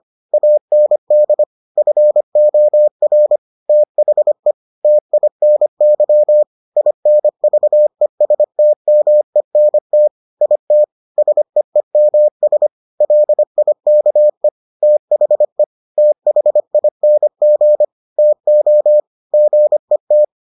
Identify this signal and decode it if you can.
Morse code, heard as a keyed tone, about 25 wpm, 600 Hz. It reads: AND FOR THE TINY INVESTMENT IT SEEMS LIKE THE THING TO GET